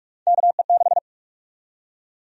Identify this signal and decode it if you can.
Morse code, heard as a keyed tone, about 44 wpm, 700 Hz. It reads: KE6
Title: USA Call Sign Prefixes - 45wpm